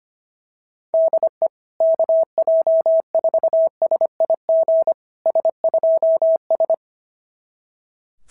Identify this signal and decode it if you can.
Morse code, heard as a keyed tone, about 24 wpm, 650 Hz. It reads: DE KJ4SIG S2S
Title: SOTA QSO Practice 25wpm